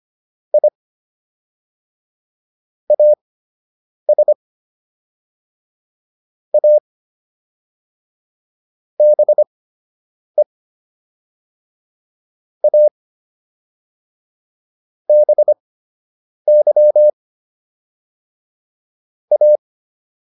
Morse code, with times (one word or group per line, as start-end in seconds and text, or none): 0.5-0.7 I
2.9-4.3 AS
6.5-6.8 A
9.0-10.4 BE
12.6-12.9 A
15.1-17.1 BY
19.3-19.6 A